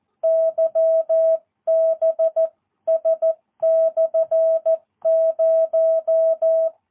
Morse code, heard as a keyed tone, about 14 wpm, 650 Hz. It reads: YBS/0
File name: jh8cbh / 2022cw